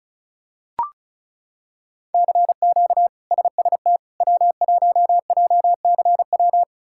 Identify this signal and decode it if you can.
Morse code, heard as a keyed tone, about 36 wpm, 700 Hz. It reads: CQ SST W1JCW
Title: K1USN SST QSO Practice - 35wpm